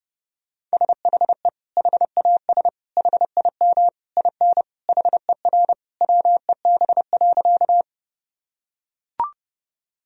Morse code, heard as a keyed tone, about 30 wpm, 700 Hz. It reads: SHE HAS HIM IN HER WEB.
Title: Sentences from Top 1000 Words - Review Part 4 - 30wpm